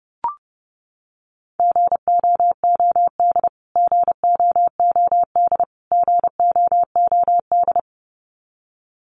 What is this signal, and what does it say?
Morse code, heard as a keyed tone, about 30 wpm, 700 Hz. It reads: GOOD GOOD GOOD
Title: Top 300 Words Repeated 30wpm